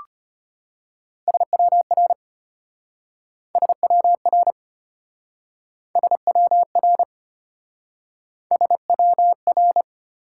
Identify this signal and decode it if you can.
Morse code, heard as a keyed tone, about 34 wpm, 700 Hz. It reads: SWR SWR SWR SWR